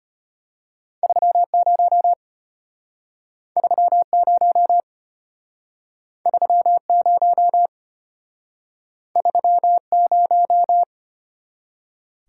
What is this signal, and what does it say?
Morse code, heard as a keyed tone, about 34 wpm, 700 Hz. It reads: 30 30 30 30